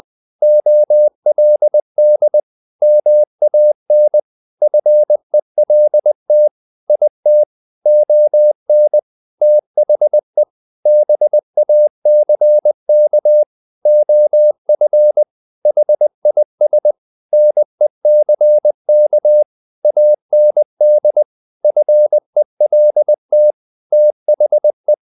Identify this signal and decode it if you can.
Morse code, heard as a keyed tone, about 20 wpm, 600 Hz. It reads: OLD MAN FELT IT ON THE BACK OF HIS NECK AND FELT THE
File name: the old man and the sea / the old man and the sea part 1 20wpm